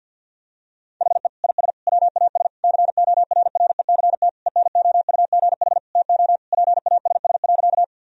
Morse code, heard as a keyed tone, about 49 wpm, 700 Hz. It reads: HE IS WAS CORRECT ENOUGH TO PASS.
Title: Sentences from Top 1000 Words - Review Part 2 - 50wpm